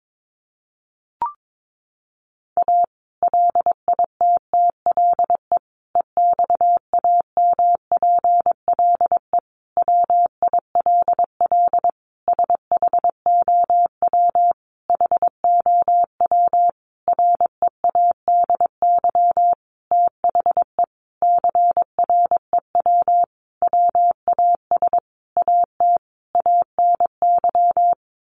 Morse code, heard as a keyed tone, about 22 wpm, 700 Hz. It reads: A LITTLE EXAMPLE WILL SHOW HOW READY THE CREW WAS AT ANY
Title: Sentences from Top 1500 Words 22wpm